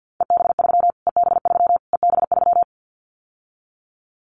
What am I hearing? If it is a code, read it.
Morse code, heard as a keyed tone, about 50 wpm, 700 Hz. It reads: EB3 EB3 EB3
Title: Callsign Prefixes for Top 25 DXCC - Repeated 3x - 50wpm